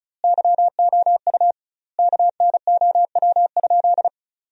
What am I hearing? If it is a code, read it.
Morse code, heard as a keyed tone, about 34 wpm, 700 Hz. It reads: YOU KNOW?
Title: 2x Word-Spacing - Sentences from Top 200 Words - 35wpm